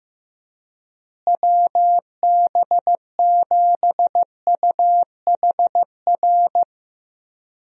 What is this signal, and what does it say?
Morse code, heard as a keyed tone, about 15 wpm, 700 Hz. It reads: WB7UHR
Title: USA Call Signs - 6 Characters - Repeated 3x - 15wpm